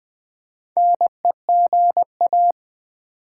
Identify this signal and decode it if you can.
Morse code, heard as a keyed tone, about 20 wpm, 700 Hz. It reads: NEGA
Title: Build A Word - 2 Letters At A Time - 8-Letter Words - 20wpm